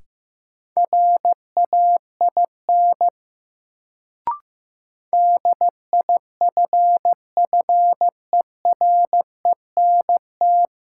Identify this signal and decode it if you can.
Morse code, heard as a keyed tone, about 15 wpm, 700 Hz. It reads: RAIN DIFFERENT
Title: Sentences from Top 600 Words - New Word Drill - 15wpm